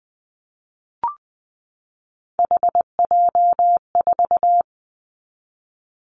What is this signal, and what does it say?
Morse code, heard as a keyed tone, about 20 wpm, 700 Hz. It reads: HJ4
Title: Callsign Prefixes for Top 100 DXCC - 20wpm